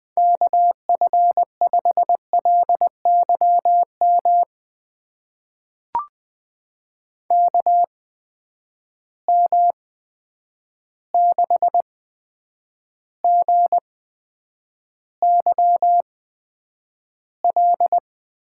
Morse code, heard as a keyed tone, about 20 wpm, spaced as for 9 wpm, 700 Hz. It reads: KF5LYM KM6GYL
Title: USA Call Signs - 6 Characters - 8x Character Spacing - 20wpm